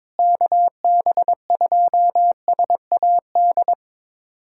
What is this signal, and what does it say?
Morse code, heard as a keyed tone, about 22 wpm, 700 Hz. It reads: KB2SAD
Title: USA Callsign DE Callsign 22wpm